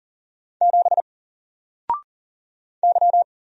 Morse code, heard as a keyed tone, about 40 wpm, 700 Hz. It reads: 7 Y
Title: Single Letter-Number 40wpm